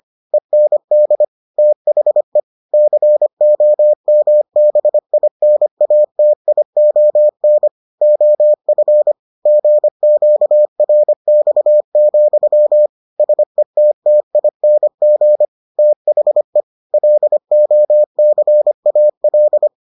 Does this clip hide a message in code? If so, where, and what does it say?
Yes: Morse code from 0.3 to 19.7 s, heard as a keyed tone, about 25 wpm, 600 Hz. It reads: END THE COMBINATION OF GQRX, SETTING THE LOCAL